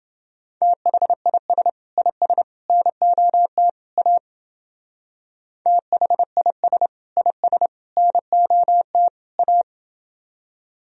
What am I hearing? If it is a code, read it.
Morse code, heard as a keyed tone, about 30 wpm, 700 Hz. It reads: THIS IS NOT A THIS IS NOT A